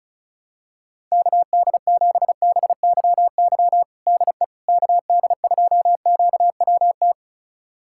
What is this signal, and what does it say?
Morse code, heard as a keyed tone, about 36 wpm, 700 Hz. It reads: KD7BYY DE KD2QWT
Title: USA Callsign DE Callsign - 6 Characters - 35wpm